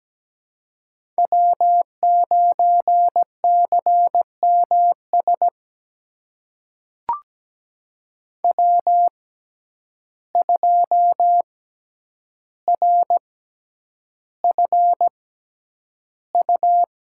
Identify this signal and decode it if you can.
Morse code, heard as a keyed tone, about 17 wpm, spaced as for 9 wpm, 700 Hz. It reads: W9CMS W2RFU
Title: USA Call Signs - 5 Characters - 6x Character Spacing - 17wpm